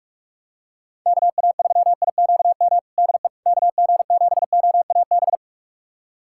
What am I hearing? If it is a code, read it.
Morse code, heard as a keyed tone, about 41 wpm, 700 Hz. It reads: KA3IQM DE KG7OAB
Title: USA Callsign DE Callsign - 6 Characters - 45wpm